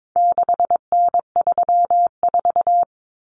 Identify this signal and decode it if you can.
Morse code, heard as a keyed tone, about 22 wpm, 700 Hz. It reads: 6N34
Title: Common Cut Numbers - 4-Digit 22wpm